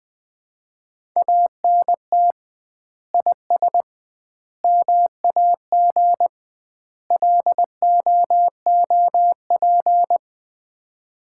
Morse code, heard as a keyed tone, about 20 wpm, 700 Hz. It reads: ANT IS MAG LOOP
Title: Common QSO Phrases - 2x Word-Spacing - 20wpm